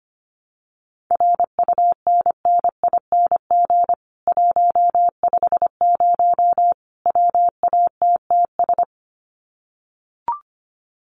Morse code, heard as a keyed tone, about 25 wpm, 700 Hz. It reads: RUNNING 150 WATTS
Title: Common QSO Phrases 25wpm